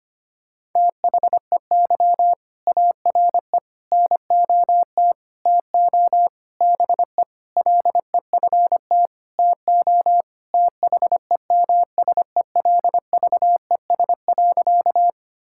Morse code, heard as a keyed tone, about 25 wpm, 700 Hz. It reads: THEY ARE NOT TO BE LEFT TO THEMSELVES.
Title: Sentences from Top 1000 Words - Review Part 5 - 25wpm